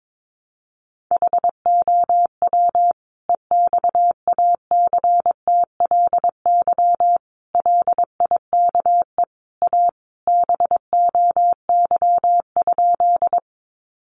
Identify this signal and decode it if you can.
Morse code, heard as a keyed tone, about 22 wpm, 700 Hz. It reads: HOW EXACTLY LIKE A BOY?
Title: Sentences from Top 900 Words 22wpm